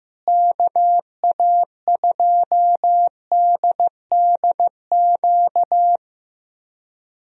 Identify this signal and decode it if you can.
Morse code, heard as a keyed tone, about 15 wpm, 700 Hz. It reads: KA2DDQ
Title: USA Call Signs - 6 Characters - Repeated 3x - 15wpm